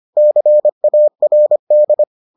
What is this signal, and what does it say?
Morse code, heard as a keyed tone, about 25 wpm, 600 Hz. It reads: CARD